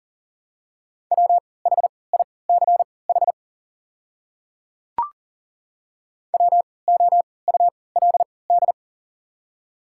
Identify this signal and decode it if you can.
Morse code, heard as a keyed tone, about 39 wpm, spaced as for 22 wpm, 690 Hz. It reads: WHICH WOULD
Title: 3x Character Spacing - Top 100 Words - 40wpm